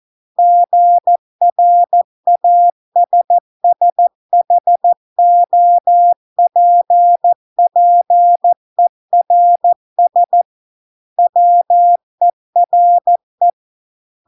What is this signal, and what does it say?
Morse code, heard as a keyed tone, about 14 wpm, 700 Hz. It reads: GRASSHOPPERS WERE